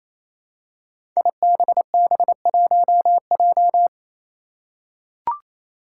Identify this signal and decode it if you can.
Morse code, heard as a keyed tone, about 29 wpm, 700 Hz. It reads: IBB1J
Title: Five Challenging Letter-Number Combinations 28wpm